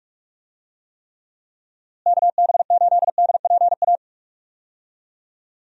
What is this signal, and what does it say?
Morse code, heard as a keyed tone, about 46 wpm, 700 Hz. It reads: KB8DPA
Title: USA Call Signs - 6 Characters - Repeated 2x - 45wpm